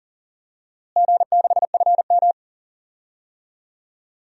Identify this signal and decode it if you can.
Morse code, heard as a keyed tone, about 40 wpm, 700 Hz. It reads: G6FM
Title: DXCC-100 Call Signs - 4 Characters - Repeated 3x - 40wpm